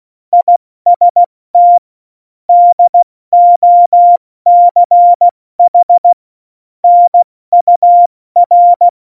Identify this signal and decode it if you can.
Morse code, heard as a keyed tone, about 16 wpm, spaced as for 14 wpm, 700 Hz. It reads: IST DOCH NUR
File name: Morsekurs / Witze / Witz 05 Mai, 2020 70 F80